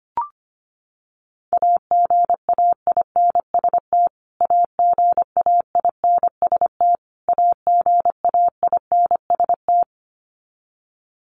Morse code, heard as a keyed tone, about 25 wpm, 700 Hz. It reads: AGAINST AGAINST AGAINST
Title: Top 500 Words Repeated 25wpm